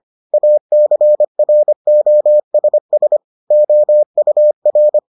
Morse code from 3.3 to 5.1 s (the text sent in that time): OUR